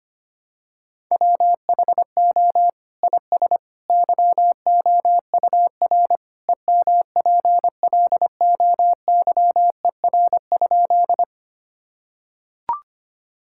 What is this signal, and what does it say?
Morse code, heard as a keyed tone, about 25 wpm, 700 Hz. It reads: WHO IS YOUR EMPLOYER?
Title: Sentences from Top 1600 Words 25wpm